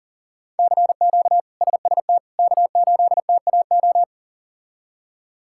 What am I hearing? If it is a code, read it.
Morse code, heard as a keyed tone, about 37 wpm, 700 Hz. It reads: CQ SST K8TAO
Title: K1USN SST QSO Practice - 40wpm